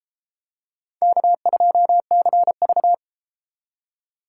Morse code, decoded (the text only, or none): K2CV